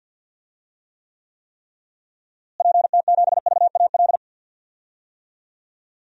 Morse code, heard as a keyed tone, about 52 wpm, 700 Hz. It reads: PT7VAL
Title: DXCC-100 Call Signs - 6 Characters - Repeated 3x - 50wpm